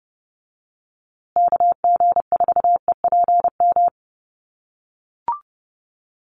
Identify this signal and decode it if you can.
Morse code, heard as a keyed tone, about 30 wpm, 700 Hz. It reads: KG4EPM
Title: USA Call Signs Repeated 2x 30wpm